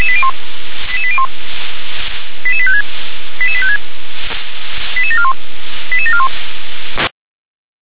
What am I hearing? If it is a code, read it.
Five-tone selective call (ZVEI), heard as quick runs of five tones, tones of about 70 ms. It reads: 88801 88856 88856 88741 88741